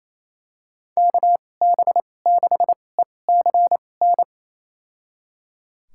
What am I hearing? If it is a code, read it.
Morse code, heard as a keyed tone, about 28 wpm, spaced as for 20 wpm, 700 Hz. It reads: KB6ECN
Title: USA Call Signs - 6 Characters - 2x Character Spacing - 28wpm